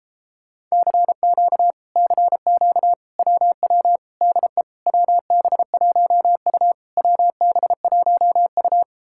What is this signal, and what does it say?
Morse code, heard as a keyed tone, about 38 wpm, 700 Hz. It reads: CQ CQ WW DE WB1U WB1U